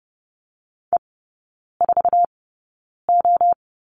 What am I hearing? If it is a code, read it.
Morse code, heard as a keyed tone, about 30 wpm, spaced as for 9 wpm, 700 Hz. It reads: E4O
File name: Lesson 19e - Letter w - Three Characters 30wpm